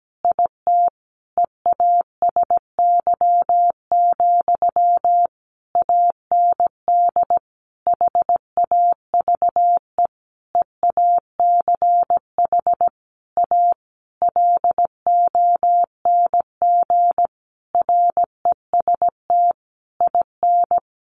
Morse code, heard as a keyed tone, about 17 wpm, 700 Hz. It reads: IT EASY, AND HAVE EACH A LONG REST IN
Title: Sentences from Top 700 Words 17wpm